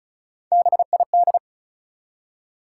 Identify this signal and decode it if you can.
Morse code, heard as a keyed tone, about 34 wpm, 700 Hz. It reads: BID
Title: Three Letter Words - 35wpm